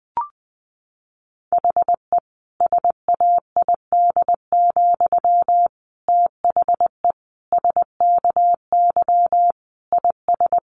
Morse code, heard as a keyed tone, about 19 wpm, 700 Hz. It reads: HE SAID, THE SKY IS
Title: Sentences from Top 1200 Words 20wpm